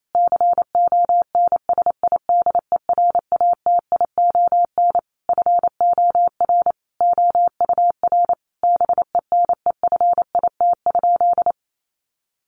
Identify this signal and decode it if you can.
Morse code, heard as a keyed tone, about 28 wpm, 700 Hz. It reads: CONSIDERATION FOR OUR BENEFIT?
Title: Sentences from Top 1500 Words 28wpm